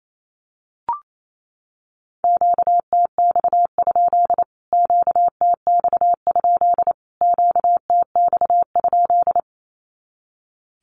Morse code, heard as a keyed tone, about 28 wpm, 700 Hz. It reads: QTX? QTX? QTX?
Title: Q-Codes Repeated 3x 28wpm